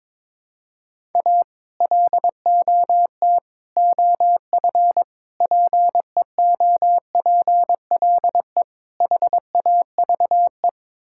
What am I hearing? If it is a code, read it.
Morse code, heard as a keyed tone, about 22 wpm, 700 Hz. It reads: A LOT OF PEOPLE HAVE
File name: practice / 22wpm / Build A Phrase - 5-Word Phrases - 22wpm